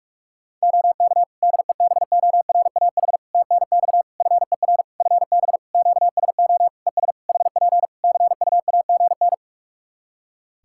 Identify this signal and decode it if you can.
Morse code, heard as a keyed tone, about 46 wpm, 700 Hz. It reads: OK DEBORAH TNX FER FB QSO ES HP CUAGN
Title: Practice QSO - Exchange 4 - Ending - 45wpm